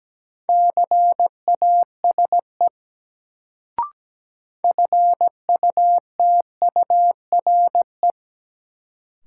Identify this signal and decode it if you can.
Morse code, heard as a keyed tone, about 17 wpm, 700 Hz. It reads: CASE FUTURE
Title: Sentences from Top 700 Words - New Word Drill - 17wpm